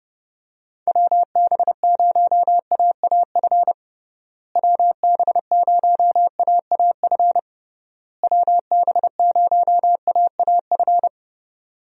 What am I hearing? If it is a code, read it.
Morse code, heard as a keyed tone, about 30 wpm, 700 Hz. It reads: WB0AAF WB0AAF WB0AAF
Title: USA Call Signs - 6 Characters - Repeated 3x - 30wpm